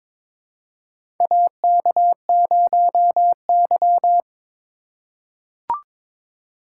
Morse code, heard as a keyed tone, about 22 wpm, 700 Hz. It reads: AK0Y
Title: USA Call Signs - 4 Characters - 2x Character Spacing - 22wpm